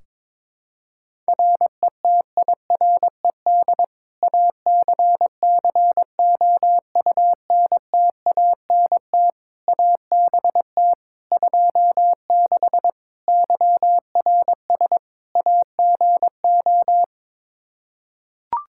Morse code, heard as a keyed tone, about 21 wpm, 700 Hz. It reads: RETIRED ACCOUNTANT ABT 26 YRS AGO
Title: Practice QSO - Exchange 3 - More Info - 22wpm